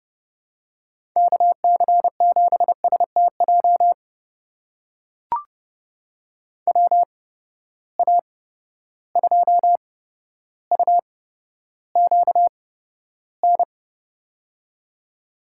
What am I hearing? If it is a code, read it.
Morse code, heard as a keyed tone, about 31 wpm, spaced as for 13 wpm, 700 Hz. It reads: KC7STJ WA2UQN
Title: USA Call Signs - 6 Characters - 8x Character Spacing - 30wpm